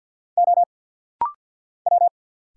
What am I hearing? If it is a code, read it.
Morse code, heard as a keyed tone, about 49 wpm, 700 Hz. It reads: O W
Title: Single Letter-Number Alt - Rapid-Fire 50wpm